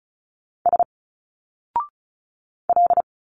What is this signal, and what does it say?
Morse code, heard as a keyed tone, about 36 wpm, 700 Hz. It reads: S L
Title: Single Letters 35wpm